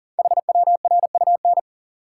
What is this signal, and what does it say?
Morse code, heard as a keyed tone, about 39 wpm, 700 Hz. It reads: HWRUN
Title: Code Groups - 5 Letters - 40wpm